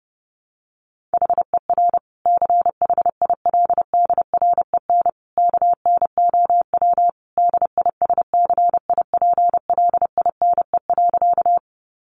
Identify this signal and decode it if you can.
Morse code, heard as a keyed tone, about 30 wpm, 700 Hz. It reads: HER CHILDREN KNOW DISCIPLINE.
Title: Sentences from Top 900 Words 30wpm